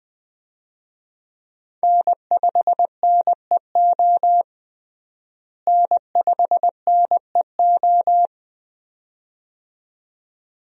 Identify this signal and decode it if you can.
Morse code, heard as a keyed tone, about 20 wpm, 700 Hz. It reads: N5NEO N5NEO
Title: USA Call Signs - 5 Characters - Repeated 2x - 20wpm